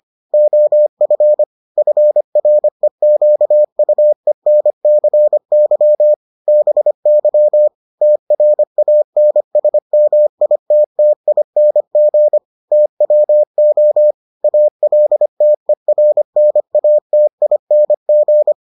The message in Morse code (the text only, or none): OF FREQUENCY BY TRANSMITTING TWO ALTERNATING